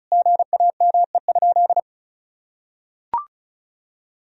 Morse code, heard as a keyed tone, about 34 wpm, 700 Hz. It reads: GAME?
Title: Sentences from Top 1000 Words - Review Part 4 - 35wpm